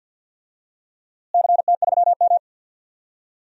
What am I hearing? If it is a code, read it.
Morse code, heard as a keyed tone, about 46 wpm, 700 Hz. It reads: CT3M
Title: DXCC-100 Call Signs - 4 Characters - Repeated 2x - 50wpm